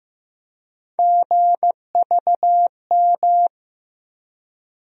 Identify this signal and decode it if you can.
Morse code, heard as a keyed tone, about 16 wpm, 700 Hz. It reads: GVM
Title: USA Call Sign Postfixes - 3 Letters 15wpm